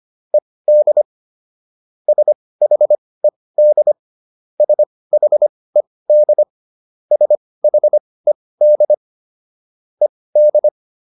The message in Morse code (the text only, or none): ED SHED SHED SHED ED